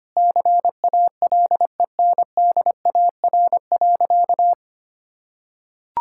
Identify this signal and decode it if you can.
Morse code, heard as a keyed tone, about 25 wpm, 700 Hz. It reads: CALENDAR.
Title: Sentences from Top 1600 Words 25wpm